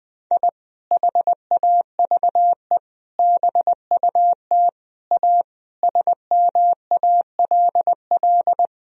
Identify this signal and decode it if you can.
Morse code, heard as a keyed tone, about 20 wpm, 700 Hz. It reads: I HAVE BUT A SMALL
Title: Sentences from Top 1000 Words - Review Part 4 - 20wpm